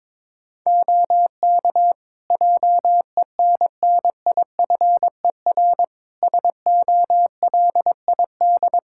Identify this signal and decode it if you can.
Morse code, heard as a keyed tone, about 22 wpm, 700 Hz. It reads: OK JENNIFER SOLID
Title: Practice QSO - Exchange 3 - More Info - 22wpm